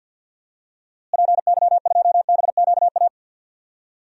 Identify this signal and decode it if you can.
Morse code, heard as a keyed tone, about 52 wpm, 700 Hz. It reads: PY2BQA
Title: DXCC-100 Call Signs - 6 Characters - Repeated 3x - 50wpm